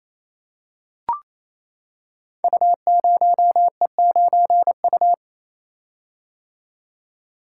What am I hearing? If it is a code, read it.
Morse code, heard as a keyed tone, about 27 wpm, 700 Hz. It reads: U0E9U
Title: Code Groups - 5 Letter:Numbers - 28wpm